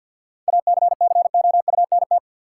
Morse code, heard as a keyed tone, about 52 wpm, 700 Hz. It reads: ACCOUNT
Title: Sentences from Top 900 Words - New Word Drill - 50wpm